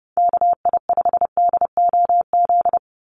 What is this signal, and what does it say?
Morse code, heard as a keyed tone, about 28 wpm, 700 Hz. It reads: KI5DOZ
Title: USA Call Signs 30wpm